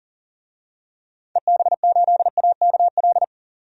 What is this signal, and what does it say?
Morse code, heard as a keyed tone, about 40 wpm, 700 Hz. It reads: EB8AKL
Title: DXCC-100 Call Signs - 6 Characters - 40wpm